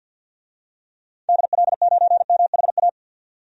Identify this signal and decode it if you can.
Morse code, heard as a keyed tone, about 46 wpm, 700 Hz. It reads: DL9MHA
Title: DXCC-100 Call Signs - 6 Characters - 50wpm